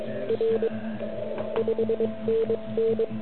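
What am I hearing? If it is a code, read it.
Morse code, heard as a keyed tone, about 21 wpm, 470 Hz, with noise. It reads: R 5NN
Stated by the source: Morse code from the OK QRP zavod 2018 contest, Chrudim.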